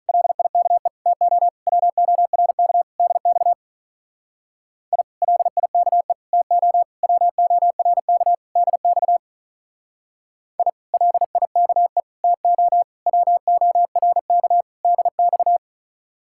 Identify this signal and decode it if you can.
Morse code, heard as a keyed tone, about 43 wpm, 700 Hz. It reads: LIKE TO WORK DX I LIKE TO WORK DX I LIKE TO WORK DX
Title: Speed-Racing Common QSO Phrases - 35wpm